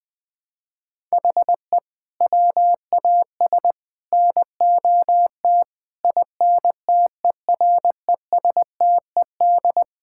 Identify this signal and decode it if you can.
Morse code, heard as a keyed tone, about 20 wpm, 700 Hz. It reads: HE WAS NOT INTERESTED
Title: Sentences from Top 1000 Words - Review Part 6 - 20wpm